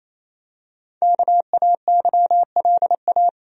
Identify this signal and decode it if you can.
Morse code, heard as a keyed tone, about 28 wpm, 700 Hz. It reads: KAYLA
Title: USA Common Names - Max 5 Letters - 28wpm